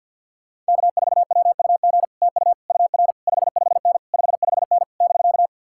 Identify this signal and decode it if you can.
Morse code, heard as a keyed tone, about 64 wpm, 700 Hz. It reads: K4WUG TU UR 55N 55N <BK>